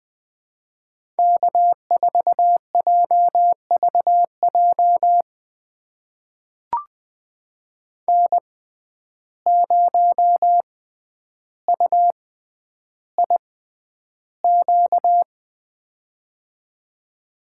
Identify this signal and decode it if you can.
Morse code, heard as a keyed tone, about 21 wpm, spaced as for 10 wpm, 700 Hz. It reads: K4JVJ N0UIQ
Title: USA Call Signs - 5 Characters - 6x Character Spacing - 20wpm